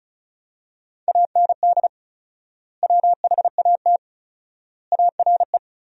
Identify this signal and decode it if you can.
Morse code, heard as a keyed tone, about 36 wpm, 700 Hz. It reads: AND WHAT ARE
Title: 4x Word-Spacing - Sentences from Top 200 Words - 35wpm